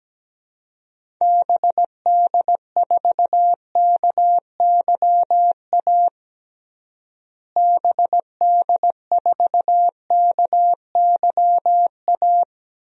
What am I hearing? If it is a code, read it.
Morse code, heard as a keyed tone, about 17 wpm, 700 Hz. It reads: BD4KYA BD4KYA
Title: DXCC-100 Call Signs - 6 Characters - Repeated 2x - 17wpm